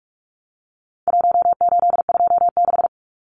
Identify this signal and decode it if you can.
Morse code, heard as a keyed tone, about 44 wpm, 700 Hz. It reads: 1826
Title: Numbers - Contest Style - 4-Digit 45wpm